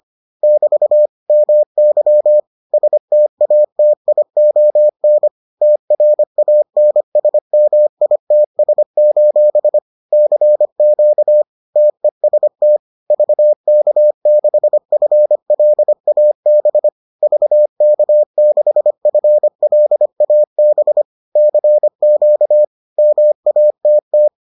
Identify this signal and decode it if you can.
Morse code, heard as a keyed tone, about 25 wpm, 600 Hz. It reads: <BT> MY STATION TRANSMITS: CQ TEST VK6FLAB VK6FLAB CQ MATT